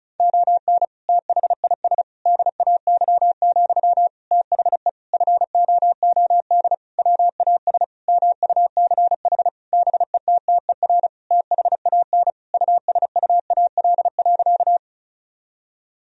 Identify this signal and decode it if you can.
Morse code, heard as a keyed tone, about 34 wpm, 700 Hz. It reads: ON THIS DAY, THE FOOD WAS MUCH BETTER THAN USUAL.
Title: Sentences from Top 1000 Words - Review Part 2 - 35wpm